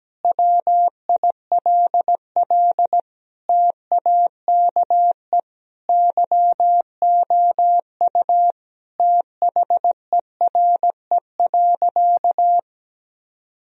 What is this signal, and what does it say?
Morse code, heard as a keyed tone, about 17 wpm, 700 Hz. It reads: WILL TAKE YOU THERE.
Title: Sentences from Top 1000 Words - Review Part 3 - 17wpm